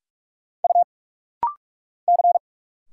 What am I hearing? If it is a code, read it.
Morse code, heard as a keyed tone, about 46 wpm, 700 Hz. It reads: U C
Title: Single Letter-Number Alt - Rapid-Fire 45wpm